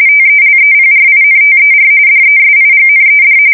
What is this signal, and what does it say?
RTTY, about 45.45 baud. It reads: TELEPRINTER DESIGNED
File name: fsk170 / teletype-corp